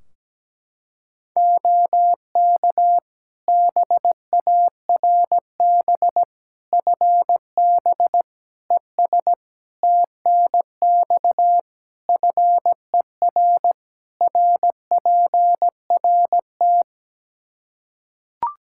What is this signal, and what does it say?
Morse code, heard as a keyed tone, about 17 wpm, 700 Hz. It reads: OK BARB FB ES TNX FER RPRT
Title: Practice QSO - Exchange 2 - Info - 17wpm